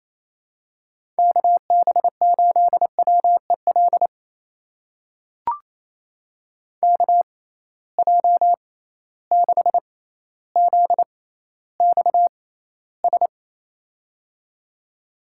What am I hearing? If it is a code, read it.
Morse code, heard as a keyed tone, about 28 wpm, spaced as for 14 wpm, 700 Hz. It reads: KB8WEL KJ6ZXS
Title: USA Call Signs - 6 Characters - 6x Character Spacing - 28wpm